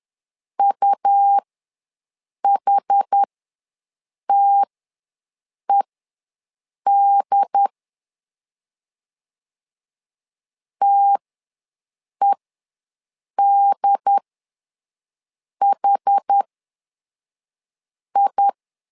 Morse code, heard as a keyed tone, about 10 wpm, spaced as for 5 wpm, 800 Hz. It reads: UHTED TEDHI
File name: LEZIONI / 2 UHDN / 2 UHD ws 10wpm 2058